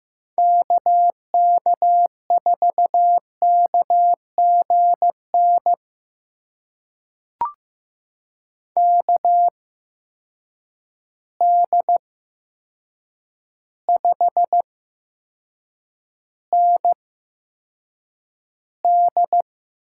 Morse code, heard as a keyed tone, about 15 wpm, spaced as for 7 wpm, 700 Hz. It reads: KK4KGN KD5ND
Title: USA Call Signs - 6 Characters - 8x Character Spacing - 15wpm